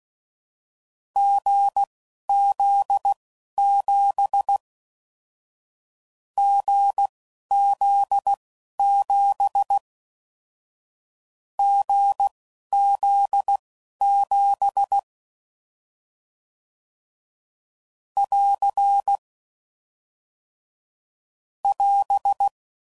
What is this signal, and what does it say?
Morse code, heard as a keyed tone, about 16 wpm, spaced as for 11 wpm, 800 Hz. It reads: GZ7 GZ7 GZ7 <AR> <AS>